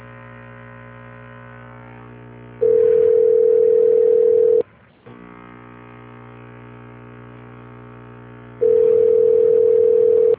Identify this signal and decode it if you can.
Ringback tone (North America), 440 + 480 Hz.
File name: rg-1600-unknown- - - .13408